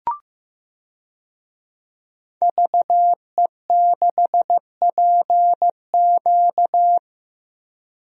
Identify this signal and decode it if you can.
Morse code, heard as a keyed tone, about 15 wpm, 700 Hz. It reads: VE6PQ
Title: DXCC-100 Call Signs - 5 Characters - Repeated 3x - 15wpm